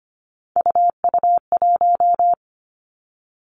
Morse code, heard as a keyed tone, about 25 wpm, 700 Hz. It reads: UU1
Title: Three Challenging Letter-Number Combinations 25wpm